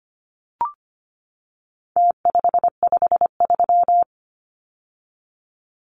Morse code, heard as a keyed tone, about 25 wpm, 700 Hz. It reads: T553
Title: Numbers - Contest Style - 4-Digit 25wpm